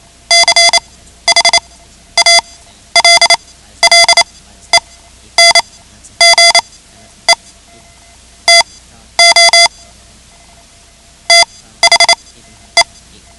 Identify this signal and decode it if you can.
Morse code, heard as a keyed tone, about 28 wpm, spaced as for 12 wpm, 2050 Hz. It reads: CHALLENGE TO THE
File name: ham radio / CodeQuick / Code Quick 10